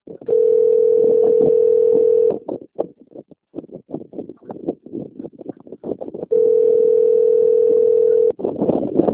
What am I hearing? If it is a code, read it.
Ringback tone (North America), 440 + 480 Hz.
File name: rg-1600-unknown- - - .28977